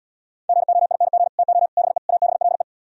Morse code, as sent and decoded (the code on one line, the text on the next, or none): --. --- .. -. --. / - --- / -... / .- -... .-.. .
GOING TO BE ABLE